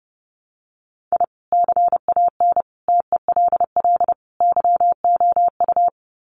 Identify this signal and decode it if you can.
Morse code, heard as a keyed tone, about 30 wpm, 700 Hz. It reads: I CAN TELL YOU
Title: Sets of 4 Words 30wpm